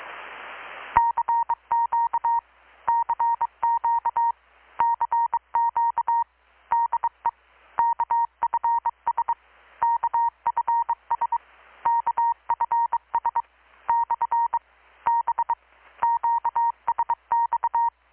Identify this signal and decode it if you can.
Morse code, heard as a keyed tone, about 23 wpm, 930 Hz. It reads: CQ CQ CQ DE KFS KFS KFS / B QSX
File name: blog / swl / 6-abril-2019-12695-USB-KPS